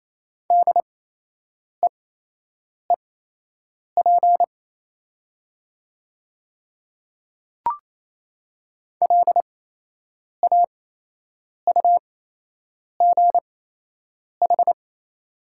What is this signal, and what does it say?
Morse code, heard as a keyed tone, about 28 wpm, spaced as for 8 wpm, 700 Hz. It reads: DEEP LAUGH
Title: 8x Character Spacing - Top 500 Words - 28wpm